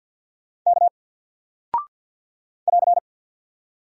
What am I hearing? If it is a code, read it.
Morse code, heard as a keyed tone, about 50 wpm, 700 Hz. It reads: K <AR>
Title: Single Character (Letter:Number:Punctuation:Prosign) Alt 50wpm